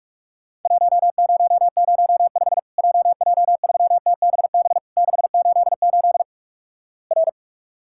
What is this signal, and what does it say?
Morse code, heard as a keyed tone, about 46 wpm, 700 Hz. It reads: 1005 JJ3TBB 688 R
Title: Worked All Europe DX Contest - QTC Exchange - 45wpm